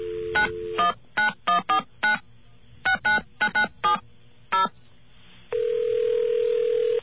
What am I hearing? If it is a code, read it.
DTMF, heard as telephone touch tones, tones of about 120 ms. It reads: 919149 69997 0